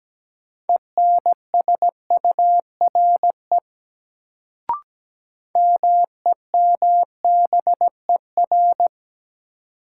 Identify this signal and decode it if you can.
Morse code, heard as a keyed tone, about 17 wpm, 700 Hz. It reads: ENSURE MEMBER
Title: Sentences from Top 1100 Words - New Word Drill - 17wpm